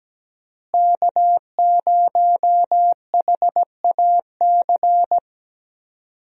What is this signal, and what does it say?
Morse code, heard as a keyed tone, about 17 wpm, 700 Hz. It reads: K0HAC
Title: USA Call Signs - 5 Characters - Repeated 3x - 17wpm